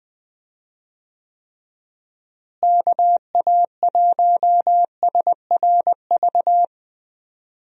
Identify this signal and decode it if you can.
Morse code, heard as a keyed tone, about 20 wpm, 700 Hz. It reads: KA1SRV
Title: USA Call Signs - 6 Characters - Repeated 3x - 20wpm